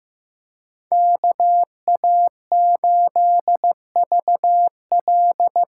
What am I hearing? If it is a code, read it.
Morse code, heard as a keyed tone, about 15 wpm, 700 Hz. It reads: KA8VL
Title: USA Call Signs - 6 Characters - 8x Character Spacing - 15wpm